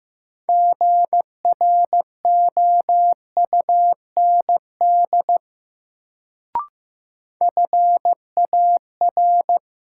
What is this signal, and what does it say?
Morse code, heard as a keyed tone, about 15 wpm, 700 Hz. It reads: GROUND FAR
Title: Sets of 3 Words - New Word Drill - 15wpm